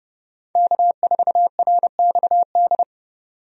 Morse code, heard as a keyed tone, about 30 wpm, 700 Hz. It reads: K4RXD
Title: USA Call Signs - 5 Characters - 30wpm